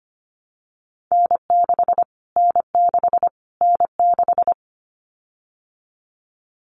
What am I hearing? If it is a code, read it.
Morse code, heard as a keyed tone, about 25 wpm, 700 Hz. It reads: N6 N6 N6
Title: Callsign Prefixes for Top 25 DXCC - Repeated 3x - 25wpm